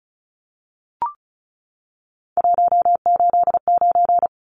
Morse code, heard as a keyed tone, about 34 wpm, 700 Hz. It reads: 189
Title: Numbers - 3-Digit 35wpm